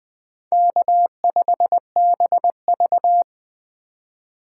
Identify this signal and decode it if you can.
Morse code, heard as a keyed tone, about 23 wpm, 700 Hz. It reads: K5BV
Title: Speed Racing - USA Call Signs 20wpm